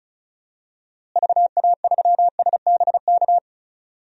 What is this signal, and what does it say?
Morse code, heard as a keyed tone, about 36 wpm, 700 Hz. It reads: VA3SBK
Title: DXCC-100 Call Signs - 6 Characters - Repeated 2x - 35wpm